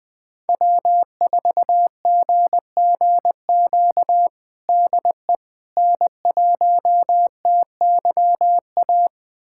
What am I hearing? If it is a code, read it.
Morse code, heard as a keyed tone, about 20 wpm, 700 Hz. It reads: W4GGQ DE N1TYA
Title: USA Callsign DE Callsign - 5 Characters - 20wpm